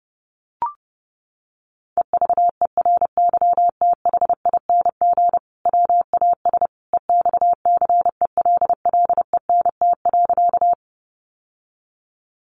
Morse code, heard as a keyed tone, about 28 wpm, 700 Hz. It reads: EVERYTHING WAS EXCELLENT.
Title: Sentences from Top 1000 Words 30wpm